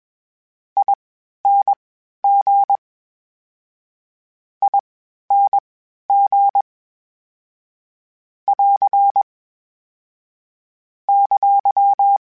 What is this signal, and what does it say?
Morse code, heard as a keyed tone, about 21 wpm, spaced as for 12 wpm, 800 Hz. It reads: ING ING <AR> !
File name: General / General-mp3s / Suffixes sp @20